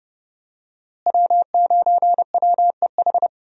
Morse code, heard as a keyed tone, about 30 wpm, 700 Hz. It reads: W9WEH
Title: USA Call Signs - 5 Characters - Repeated 2x - 30wpm